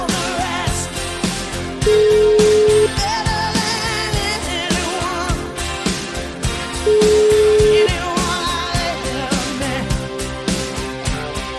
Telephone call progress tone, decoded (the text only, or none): ringback tone